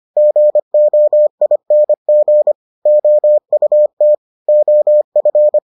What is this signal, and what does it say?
Morse code, heard as a keyed tone, about 25 wpm, 600 Hz. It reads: GOING OUT OF